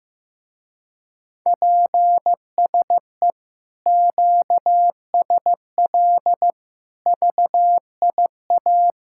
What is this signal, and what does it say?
Morse code, heard as a keyed tone, about 16 wpm, 700 Hz. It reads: PSE QSL VIA
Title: Speed-Racing Common QSO Phrases Encore - 15wpm